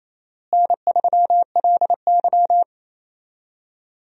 Morse code, heard as a keyed tone, about 28 wpm, 700 Hz. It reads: N3LY
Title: USA Call Signs - 4 Characters - Repeated 2x - 28wpm